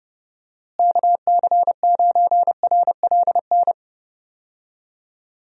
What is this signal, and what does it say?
Morse code, heard as a keyed tone, about 30 wpm, 700 Hz. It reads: KC9RLN
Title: USA Call Signs - 6 Characters - Repeated 2x - 30wpm